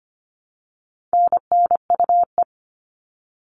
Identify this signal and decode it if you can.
Morse code, heard as a keyed tone, about 25 wpm, 700 Hz. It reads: NNUE
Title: All Cut Numbers - 4-Digit 25wpm